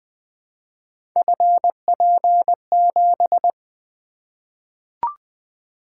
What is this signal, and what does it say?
Morse code, heard as a keyed tone, about 20 wpm, 700 Hz. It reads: FP7
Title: Three Letter-Number Combinations - 20wpm